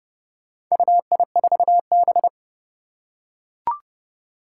Morse code, heard as a keyed tone, about 28 wpm, 700 Hz. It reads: UI4B
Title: Four Challenging Letter-Number Combinations 30wpm